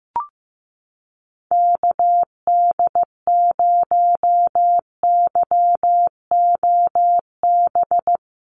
Morse code, heard as a keyed tone, about 15 wpm, 700 Hz. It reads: KD0YOB
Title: NA SOTA Activator Callsigns 15wpm